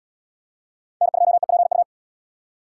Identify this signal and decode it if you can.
Morse code, heard as a keyed tone, about 72 wpm, 700 Hz. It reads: N2EPU